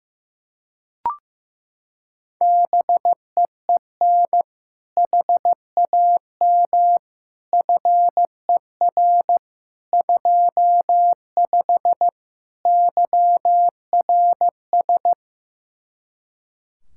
Morse code, heard as a keyed tone, about 15 wpm, 700 Hz. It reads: BEEN HAM FER 25 YRS
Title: Practice QSO - Exchange 3 - More Info - 15wpm